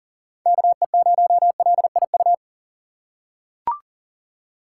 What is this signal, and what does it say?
Morse code, heard as a keyed tone, about 40 wpm, 700 Hz. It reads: KE0LIU
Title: USA Call Signs - 6 Characters - 4x Character Spacing - 40wpm